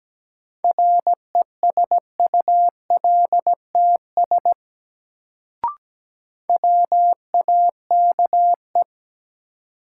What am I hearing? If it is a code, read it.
Morse code, heard as a keyed tone, about 17 wpm, 700 Hz. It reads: RESULTS WAKE
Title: Sets of 4 Words - New Word Drill - 17wpm